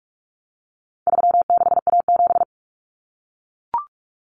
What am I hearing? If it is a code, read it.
Morse code, heard as a keyed tone, about 44 wpm, 700 Hz. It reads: 36A7
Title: Common Cut Numbers - 4-Digit 45wpm